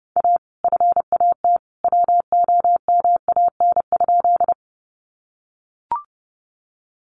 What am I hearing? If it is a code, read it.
Morse code, heard as a keyed tone, about 30 wpm, 700 Hz. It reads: A FAT WOMAN?
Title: Sentences from Top 1100 Words 30wpm